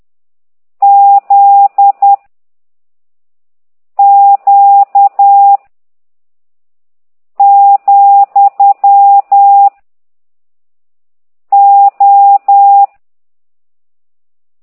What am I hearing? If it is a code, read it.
Morse code, heard as a keyed tone, about 10 wpm, 800 Hz. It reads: Z Q , O
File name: ct2idw / exerc15